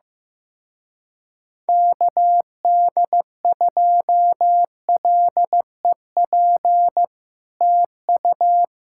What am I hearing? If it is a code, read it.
Morse code, heard as a keyed tone, about 15 wpm, 700 Hz. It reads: KD2LEP TU
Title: Speed-Racing Common QSO Phrases Encore - 15wpm